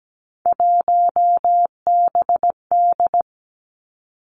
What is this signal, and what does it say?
Morse code, heard as a keyed tone, about 17 wpm, 700 Hz. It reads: 1BD
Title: Three Challenging Letter-Number Combinations 17wpm